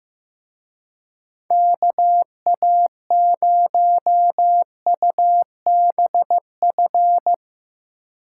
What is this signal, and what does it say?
Morse code, heard as a keyed tone, about 16 wpm, 700 Hz. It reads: KA0UBF
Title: Speed Racing - USA 2x3 Call Signs 15wpm